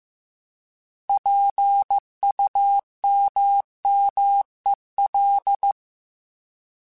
Morse code, heard as a keyed tone, about 15 wpm, 800 Hz. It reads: PUMMEL
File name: Alphabetical / Alphabetical mp3s / Ps @13